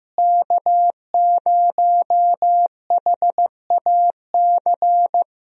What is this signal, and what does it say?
Morse code, heard as a keyed tone, about 15 wpm, 700 Hz. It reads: K0HAC
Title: USA Call Signs - 5 Characters - Repeated 3x - 15wpm